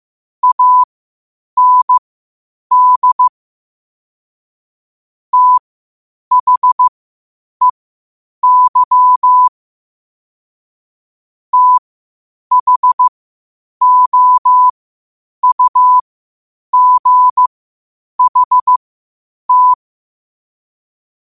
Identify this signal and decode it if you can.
Morse code, heard as a keyed tone, about 15 wpm, spaced as for 8 wpm, 1000 Hz. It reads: AND THEY THOUGHT
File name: qotdfiles / 8wpm